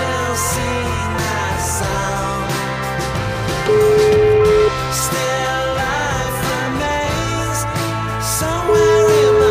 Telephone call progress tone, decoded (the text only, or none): ringback tone